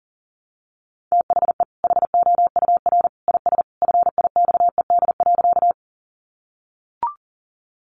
Morse code, heard as a keyed tone, about 40 wpm, 700 Hz. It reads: THE HOUR IS FIXED.
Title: Sentences from Top 1100 Words 40wpm